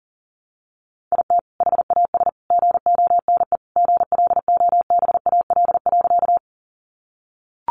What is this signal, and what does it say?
Morse code, heard as a keyed tone, about 37 wpm, 700 Hz. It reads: IT HAS GONE GLOBAL.
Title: Sentences from Top 1200 Words 40wpm